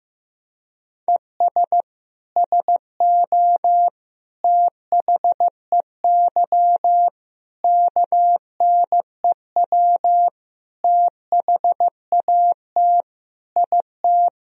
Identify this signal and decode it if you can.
Morse code, heard as a keyed tone, about 15 wpm, 700 Hz. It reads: ES SO THEY KNEW THAT IT
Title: Sentences from Top 400 Words and Abbreviations 15wpm